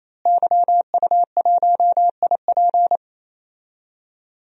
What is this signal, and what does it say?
Morse code, heard as a keyed tone, about 28 wpm, 700 Hz. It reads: YU1IP
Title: DXCC-100 Call Signs - 5 Characters - Repeated 3x - 28wpm